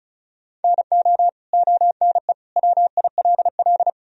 Morse code, heard as a keyed tone, about 36 wpm, 700 Hz. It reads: NO ONE WILL
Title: Sentences from Top 100 Words and Abbreviations 35wpm